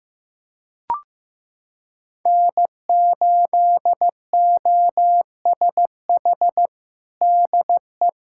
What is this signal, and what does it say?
Morse code, heard as a keyed tone, about 15 wpm, 700 Hz. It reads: N8OSH DE
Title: USA Callsign DE Callsign - 5 Characters - 15wpm